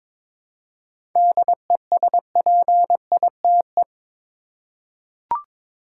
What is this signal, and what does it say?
Morse code, heard as a keyed tone, about 22 wpm, 700 Hz. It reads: DESPITE
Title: Seven Letter Words - 22wpm